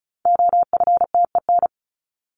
Morse code, heard as a keyed tone, about 34 wpm, 700 Hz. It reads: OFTEN
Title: Top 500 Words 35wpm